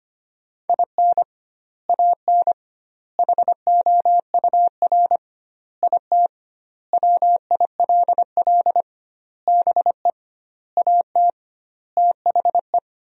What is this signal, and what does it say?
Morse code, heard as a keyed tone, about 25 wpm, 700 Hz. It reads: IN AN HOUR IT WILL BE AT THE
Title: 2x Word-Spacing - Sentences from Top 400 Words - 25wpm